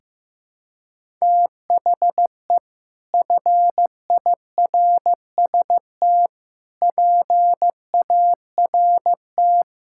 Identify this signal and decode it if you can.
Morse code, heard as a keyed tone, about 15 wpm, 700 Hz. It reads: THE FIRST PART
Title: Sentences from Top 1000 Words - Review Part 6 - 15wpm